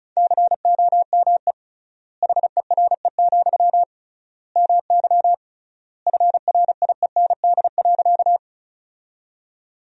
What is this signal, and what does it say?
Morse code, heard as a keyed tone, about 36 wpm, 700 Hz. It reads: COME HERE, MY FRIEND.
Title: 3x Word-Spacing - Sentences from Top 300 Words - 35wpm